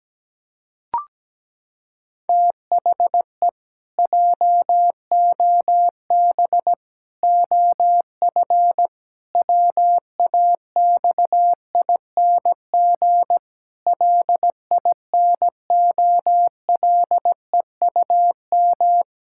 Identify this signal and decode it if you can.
Morse code, heard as a keyed tone, about 17 wpm, 700 Hz. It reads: THE JOB OF WAXING LINOLEUM
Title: Pangram Challenge v2 - 17wpm